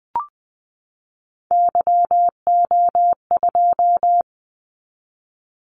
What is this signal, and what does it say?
Morse code, heard as a keyed tone, about 20 wpm, 700 Hz. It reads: YO2
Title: Callsign Prefixes for Top 25 DXCC - 20wpm